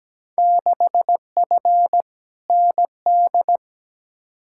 Morse code, heard as a keyed tone, about 17 wpm, 700 Hz. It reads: 6F ND
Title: Field Day Exchanges Encore - 17wpm